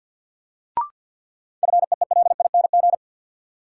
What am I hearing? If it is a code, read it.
Morse code, heard as a keyed tone, about 52 wpm, 700 Hz. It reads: FEELING
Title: Sentences from Top 1200 Words - New Word Drill Alt - 50wpm